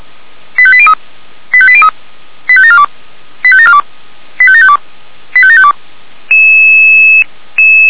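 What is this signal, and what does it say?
Five-tone selective call (ZVEI), heard as quick runs of five tones, tones of about 70 ms. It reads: 85803 85803 85742 85742 85752 85752
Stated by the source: ZVEI five-tone selective call received by SDR.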